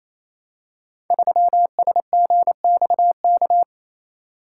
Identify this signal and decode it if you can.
Morse code, heard as a keyed tone, about 28 wpm, 700 Hz. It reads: 3SGXK
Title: Code Groups - 5 Letter:Numbers - 28wpm